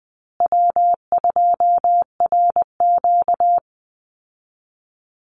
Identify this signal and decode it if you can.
Morse code, heard as a keyed tone, about 20 wpm, 700 Hz. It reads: W2RQ
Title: Call Signs Repeated 2x - 20wpm